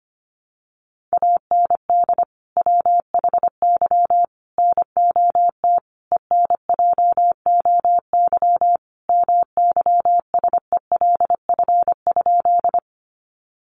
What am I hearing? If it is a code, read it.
Morse code, heard as a keyed tone, about 25 wpm, 700 Hz. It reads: AND WHY NOT ENJOY MYSELF?
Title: Sentences from Top 1300 Words 25wpm